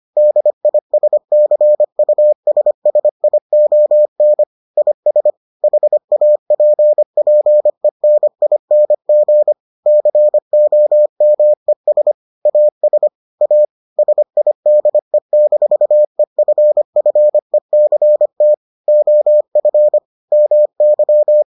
Morse code, heard as a keyed tone, about 25 wpm, 600 Hz. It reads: DISCUSSION IS HAPPENING COMES AS A SIDE-EFFECT OF MY